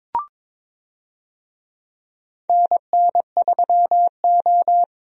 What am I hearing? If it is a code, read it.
Morse code, heard as a keyed tone, about 22 wpm, 700 Hz. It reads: NN3O
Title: USA Call Signs - 4 Characters - Repeated 3x - 22wpm